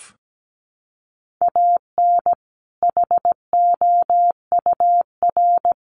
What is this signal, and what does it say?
Morse code, heard as a keyed tone, about 17 wpm, 700 Hz. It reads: AN HOUR
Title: Sentences from Top 400 Words 17wpm